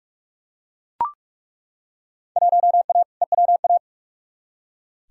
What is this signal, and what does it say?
Morse code, heard as a keyed tone, about 42 wpm, 700 Hz. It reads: 1A EWA
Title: Field Day Exchanges Encore - 45wpm